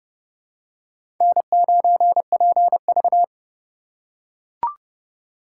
Morse code, heard as a keyed tone, about 30 wpm, 700 Hz. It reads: N9PV
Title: USA Call Signs - 4 Characters - 4x Character Spacing - 30wpm